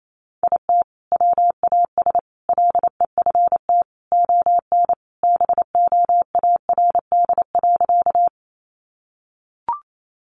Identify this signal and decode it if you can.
Morse code, heard as a keyed tone, about 28 wpm, 700 Hz. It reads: IT WAS LEFT ON BOARD.
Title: Sentences from Top 700 Words 28wpm